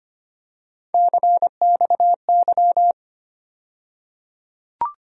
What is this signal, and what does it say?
Morse code, heard as a keyed tone, about 25 wpm, 700 Hz. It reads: CXY
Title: USA Call Sign Postfixes - 3 Letters 25wpm